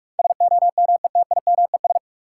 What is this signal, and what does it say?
Morse code, heard as a keyed tone, about 42 wpm, 700 Hz. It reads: SOMETIMES
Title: Sentences from Top 600 Words - New Word Drill - 45wpm